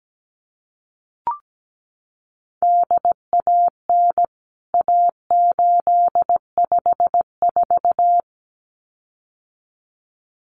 Morse code, heard as a keyed tone, about 17 wpm, 700 Hz. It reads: DAN A854
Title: CWT Exchange 17wpm